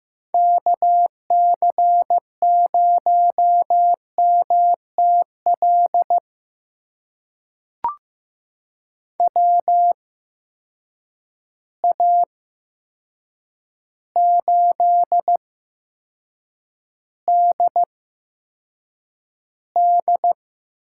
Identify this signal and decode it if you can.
Morse code, heard as a keyed tone, about 15 wpm, spaced as for 7 wpm, 700 Hz. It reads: KC0MTL WA8DD
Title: USA Call Signs - 6 Characters - 8x Character Spacing - 15wpm